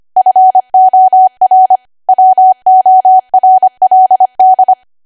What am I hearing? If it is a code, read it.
Morse code, heard as a keyed tone, about 25 wpm, 750 Hz. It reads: FOR WORLD